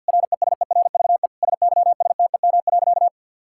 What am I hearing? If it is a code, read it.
Morse code, heard as a keyed tone, about 50 wpm, 700 Hz. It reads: RESERVE SYSTEM.
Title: Sentences from Top 1000 Words - Review Part 6 - 50wpm